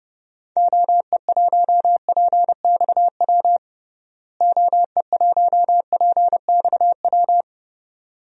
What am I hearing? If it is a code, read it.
Morse code, heard as a keyed tone, about 30 wpm, 700 Hz. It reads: OE1PXW OE1PXW
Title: DXCC-100 Call Signs - 6 Characters - Repeated 2x - 30wpm